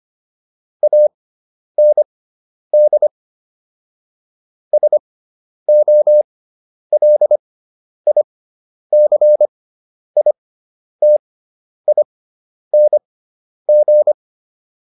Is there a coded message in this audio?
Yes: Morse code from 0.8 to 14.1 s, heard as a keyed tone, about 25 wpm, spaced as for 10 wpm, 600 Hz. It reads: AND SOLICITING